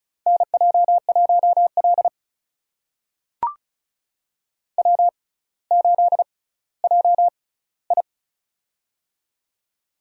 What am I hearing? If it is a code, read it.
Morse code, heard as a keyed tone, about 36 wpm, spaced as for 18 wpm, 700 Hz. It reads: NJ1L W8JI
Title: USA Call Signs - 4 Characters - 6x Character Spacing - 35wpm